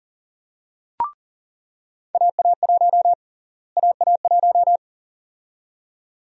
Morse code, heard as a keyed tone, about 40 wpm, 700 Hz. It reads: AA1 AA1
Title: USA Call Sign Prefixes - Repeated 2x - 40wpm